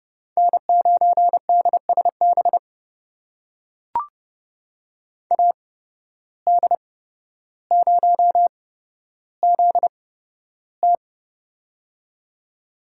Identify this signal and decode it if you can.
Morse code, heard as a keyed tone, about 29 wpm, spaced as for 13 wpm, 700 Hz. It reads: N9DSB AD0ZT
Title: USA Call Signs - 5 Characters - 8x Character Spacing - 30wpm